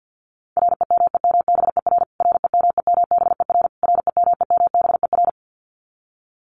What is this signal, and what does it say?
Morse code, heard as a keyed tone, about 50 wpm, 700 Hz. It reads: REMEMBER REMEMBER REMEMBER
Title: Top 500 Words Repeated 50wpm